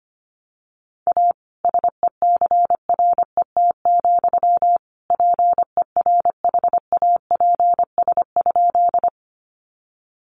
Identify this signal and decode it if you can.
Morse code, heard as a keyed tone, about 25 wpm, 700 Hz. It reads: A SECRET, PERHAPS?
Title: Sentences from Top 1200 Words 25wpm